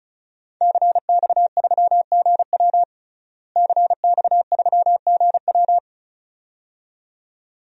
Morse code, heard as a keyed tone, about 34 wpm, 700 Hz. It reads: CX3GW CX3GW
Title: DXCC-100 Call Signs - 5 Characters - Repeated 2x - 35wpm